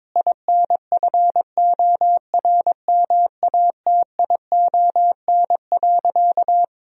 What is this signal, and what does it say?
Morse code, heard as a keyed tone, about 22 wpm, 700 Hz. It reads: INFORMATION.
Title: Sentences from Top 1000 Words - Review Part 4 - 22wpm